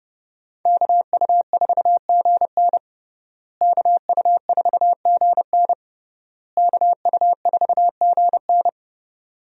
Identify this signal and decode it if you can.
Morse code, heard as a keyed tone, about 30 wpm, 700 Hz. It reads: KU4GN KU4GN KU4GN
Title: USA Call Signs - 5 Characters - Repeated 3x - 30wpm